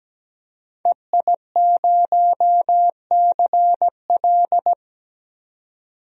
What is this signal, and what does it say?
Morse code, heard as a keyed tone, about 17 wpm, 700 Hz. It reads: EI0CL
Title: DXCC-100 Call Signs - 5 Characters - 17wpm